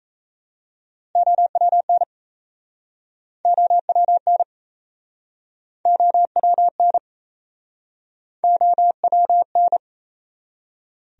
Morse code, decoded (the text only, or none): OWN OWN OWN OWN